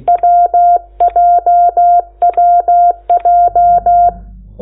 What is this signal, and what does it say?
Morse code, heard as a keyed tone, about 16 wpm, 700 Hz. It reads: WJWJ
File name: Lecciones / CLASE Nº8